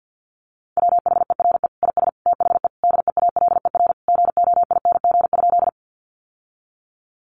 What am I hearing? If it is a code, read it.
Morse code, heard as a keyed tone, about 50 wpm, 700 Hz. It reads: WHERE IS THE DEALER GOING?
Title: Sentences from Top 1100 Words 50wpm